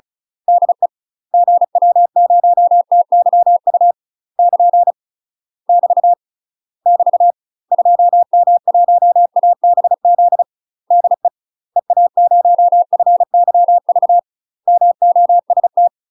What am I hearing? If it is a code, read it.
Morse code, heard as a keyed tone, about 33 wpm, 700 Hz. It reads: DE GW0TYU <KN> <BT> <BT> 2M1ABZ DE EA0FYV MOST
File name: UK MASTER QSOs TRAINING / 35WPM / UK MASTER QSOs 0002